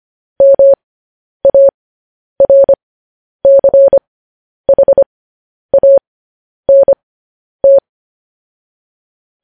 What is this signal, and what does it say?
Morse code, heard as a keyed tone, about 25 wpm, spaced as for 10 wpm, 550 Hz. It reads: MARCHANT
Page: news-headlines-in-morse-code-at-10-wpm